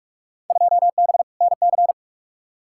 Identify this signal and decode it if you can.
Morse code, heard as a keyed tone, about 47 wpm, 700 Hz. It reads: 2B NC
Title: Field Day Exchanges Encore - 45wpm